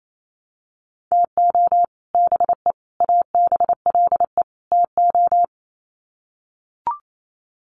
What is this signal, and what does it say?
Morse code, heard as a keyed tone, about 28 wpm, 700 Hz. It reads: TO BE ABLE TO
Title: Sets of 4 Words 28wpm